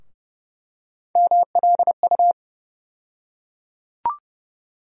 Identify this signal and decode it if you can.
Morse code, heard as a keyed tone, about 30 wpm, 700 Hz. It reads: MLU
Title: USA Call Sign Postfixes - 3 Letters 30wpm